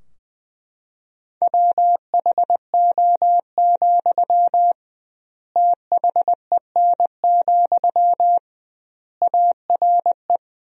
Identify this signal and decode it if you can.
Morse code, heard as a keyed tone, about 21 wpm, 700 Hz. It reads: WHO, THEN, ARE
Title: 2x Word-Spacing - Sentences from Top 100 Words - 20wpm